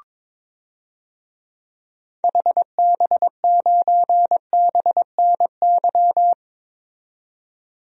Morse code, heard as a keyed tone, about 21 wpm, 700 Hz. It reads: HB9BNY
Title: DXCC-100 Call Signs - 6 Characters - 22wpm